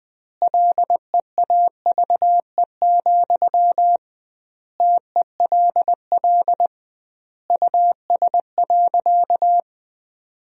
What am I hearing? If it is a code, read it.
Morse code, heard as a keyed tone, about 20 wpm, 700 Hz. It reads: LEAVE, TELL US.
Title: 2x Word-Spacing - Sentences from Top 400 Words - 20wpm